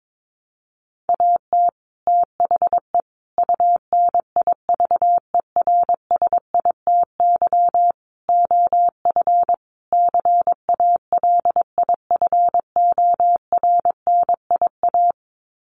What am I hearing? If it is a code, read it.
Morse code, heard as a keyed tone, about 22 wpm, 700 Hz. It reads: AT THE UNIVERSITY OF CALIFORNIA
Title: Sets of 5 Words Encore2 22wpm